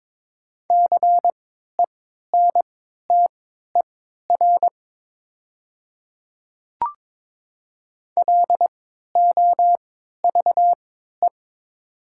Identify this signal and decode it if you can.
Morse code, heard as a keyed tone, about 22 wpm, spaced as for 13 wpm, 700 Hz. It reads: CENTER LOVE
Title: 3x Character Spacing - Top 500 Words - 22wpm